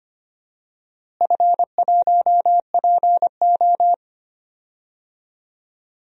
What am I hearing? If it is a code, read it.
Morse code, heard as a keyed tone, about 25 wpm, 700 Hz. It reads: F1PO
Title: DXCC-100 Call Signs - 4 Characters - Repeated 2x - 25wpm